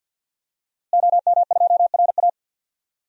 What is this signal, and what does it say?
Morse code, heard as a keyed tone, about 48 wpm, 700 Hz. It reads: OM2RA
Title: DXCC-100 Call Signs - 5 Characters - Repeated 2x - 50wpm